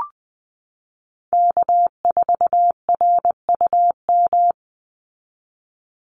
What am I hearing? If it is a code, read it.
Morse code, heard as a keyed tone, about 19 wpm, 700 Hz. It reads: K4RUM
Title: Call Signs Repeated 3x - 20wpm